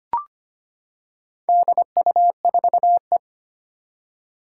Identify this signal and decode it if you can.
Morse code, heard as a keyed tone, about 25 wpm, 700 Hz. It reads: DU4E
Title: Four Challenging Letter-Number Combinations 25wpm